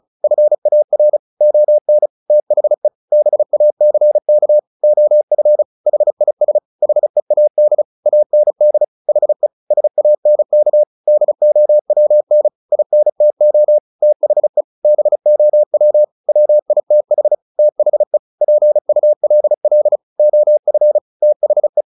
Morse code, heard as a keyed tone, about 34 wpm, 600 Hz. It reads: FAR ON THE BACK OF HIS HEAD AND HE SANK DOWN INTO THE BOW WITH THE PULL OF THE
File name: the old man and the sea 2 35wpm